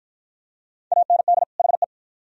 Morse code, heard as a keyed tone, about 47 wpm, 700 Hz. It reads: AND HE